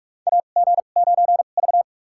Morse code, heard as a keyed tone, about 44 wpm, spaced as for 32 wpm, 700 Hz. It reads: AG9V
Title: USA Call Signs - 4 Characters - 2x Character Spacing - 45wpm